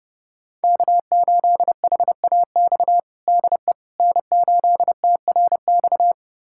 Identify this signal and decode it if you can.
Morse code, heard as a keyed tone, about 29 wpm, 700 Hz. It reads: K8HAX DE N8TRX
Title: USA Callsign DE Callsign - 5 Characters - 30wpm